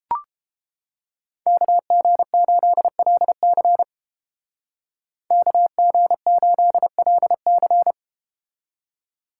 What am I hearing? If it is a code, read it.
Morse code, heard as a keyed tone, about 31 wpm, 700 Hz. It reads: KG8LC KG8LC